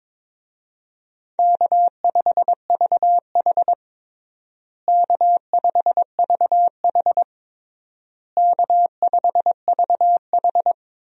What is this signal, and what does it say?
Morse code, heard as a keyed tone, about 21 wpm, 700 Hz. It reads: K5VH K5VH K5VH
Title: USA Call Signs - 4 Characters - Repeated 3x - 22wpm